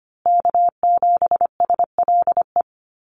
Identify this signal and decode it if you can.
Morse code, heard as a keyed tone, about 25 wpm, 700 Hz. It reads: K7SLE
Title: USA Call Signs Repeated 3x 25wpm